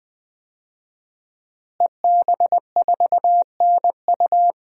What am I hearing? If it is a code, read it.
Morse code, heard as a keyed tone, about 21 wpm, 700 Hz. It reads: EB4NU
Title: DXCC-100 Call Signs - 5 Characters - Repeated 2x - 20wpm